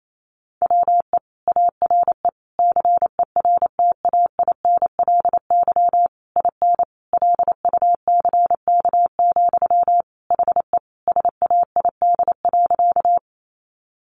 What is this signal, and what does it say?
Morse code, heard as a keyed tone, about 28 wpm, 700 Hz. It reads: WE ARE CERTAINLY IN LUCK, HE SAID.
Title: Sentences from Top 1500 Words 28wpm